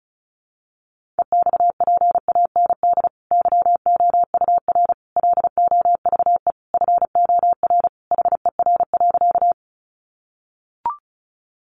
Morse code, heard as a keyed tone, about 34 wpm, 700 Hz. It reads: EXPAND YOUR LOVE FOR HER.
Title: Sentences from Top 1400 Words 35wpm